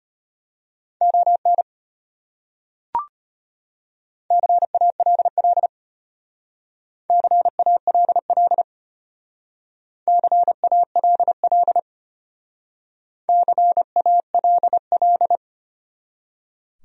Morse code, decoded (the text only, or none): ON CALL CALL CALL CALL